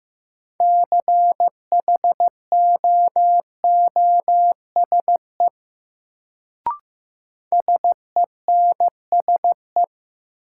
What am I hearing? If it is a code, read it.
Morse code, heard as a keyed tone, about 15 wpm, 700 Hz. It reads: CHOOSE SENSE
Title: Sentences from Top 700 Words - New Word Drill - 15wpm